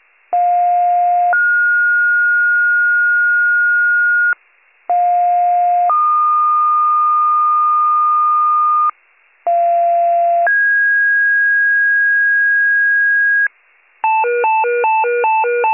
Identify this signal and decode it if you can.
Two-tone page, heard as a one-second tone, then a three-second tone, about 710 then 1430 Hz, about 710 then 1220 Hz, about 690 then 1650 Hz, with noise. It is heard from 0.3 to 13.5 s.